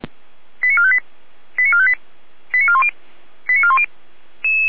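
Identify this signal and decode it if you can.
Five-tone selective call (ZVEI), heard as quick runs of five tones, tones of about 70 ms. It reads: 89458 89458 89410 89410
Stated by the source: ZVEI five-tone call 88074 received on an SDR, recorded by Feuerwehr Orlamünde.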